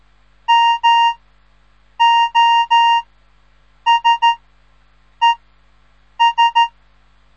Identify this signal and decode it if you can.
Morse code, heard as a keyed tone, about 13 wpm, spaced as for 7 wpm, 930 Hz. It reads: MOSES
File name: radiomorningstar / morse / les 3